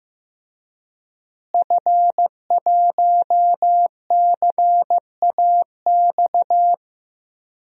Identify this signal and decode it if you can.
Morse code, heard as a keyed tone, about 15 wpm, 700 Hz. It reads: F1CAX
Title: DXCC-100 Call Signs - 5 Characters - Repeated 2x - 15wpm